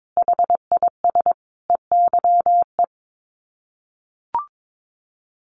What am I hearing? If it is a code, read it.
Morse code, heard as a keyed tone, about 22 wpm, 700 Hz. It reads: HIS EYE
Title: Sets of 5 Words Encore2 22wpm